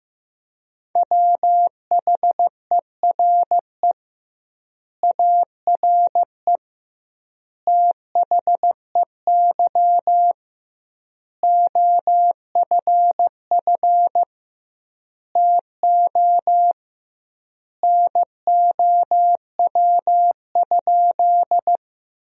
Morse code, heard as a keyed tone, about 15 wpm, 700 Hz. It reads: WHERE ARE THEY OFF TO NOW?
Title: 2x Word-Spacing - Sentences from Top 200 Words - 15wpm